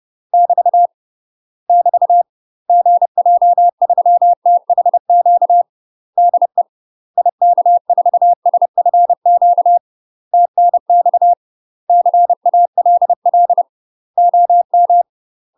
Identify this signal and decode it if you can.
Morse code, heard as a keyed tone, about 30 wpm, 700 Hz. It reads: <BT> <BT> GJ3THQ DE IK4SFQ TNX CALL OM
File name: UK MASTER QSOs TRAINING / 30WPM / UK MASTER QSOs 0019